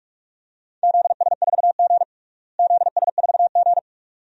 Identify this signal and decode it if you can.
Morse code, heard as a keyed tone, about 43 wpm, 700 Hz. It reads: ZS4G ZS4G
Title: DXCC-100 Call Signs - 4 Characters - Repeated 2x - 45wpm